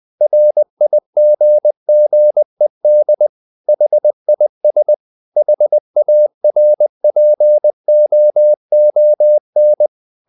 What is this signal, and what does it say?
Morse code, heard as a keyed tone, about 20 wpm, 600 Hz. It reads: RIGGED HIS HARPOON
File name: the old man and the sea part 2 20wpm